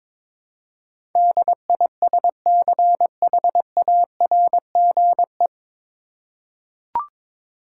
Morse code, heard as a keyed tone, about 22 wpm, 700 Hz. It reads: DISCHARGE
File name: 9 Letter Words - First 3 Letters Given - 22wpm